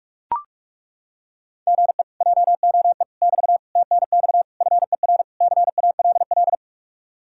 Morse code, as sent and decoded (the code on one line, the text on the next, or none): --. / .--- --- . / -...- / - -. -..- / ..-. .-. / -.-. .- .-.. .-..
GE JOE <BT> TNX FER CALL